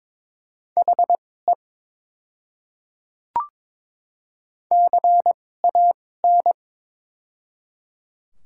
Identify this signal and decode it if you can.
Morse code, heard as a keyed tone, about 22 wpm, spaced as for 16 wpm, 700 Hz. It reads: HE CAN
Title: 2x Character Spacing - Top 200 Words - 22wpm